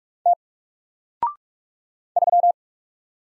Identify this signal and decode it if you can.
Morse code, heard as a keyed tone, about 44 wpm, 700 Hz. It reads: T 3
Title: Single Letter-Number Alt 45wpm